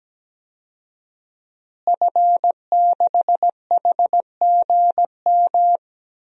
Morse code, heard as a keyed tone, about 17 wpm, 700 Hz. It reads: F6HGM
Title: DXCC-100 Call Signs - 5 Characters - Repeated 2x - 17wpm